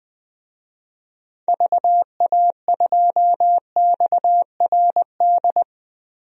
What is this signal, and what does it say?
Morse code, heard as a keyed tone, about 20 wpm, 700 Hz. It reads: VA2XRD
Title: DXCC-100 Call Signs - 6 Characters - Repeated 3x - 20wpm